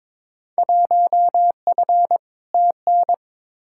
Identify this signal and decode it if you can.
Morse code, heard as a keyed tone, about 22 wpm, 700 Hz. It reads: 1F TN
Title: Field Day Exchanges - 22wpm